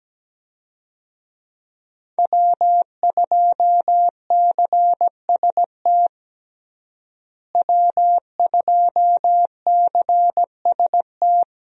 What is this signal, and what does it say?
Morse code, heard as a keyed tone, about 17 wpm, 700 Hz. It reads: W2CST W2CST
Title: USA Call Signs - 5 Characters - Repeated 2x - 17wpm